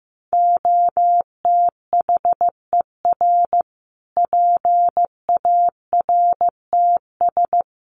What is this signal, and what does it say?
Morse code, heard as a keyed tone, about 15 wpm, 700 Hz. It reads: OTHER PARTS
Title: Sets of 5 Words Encore 15wpm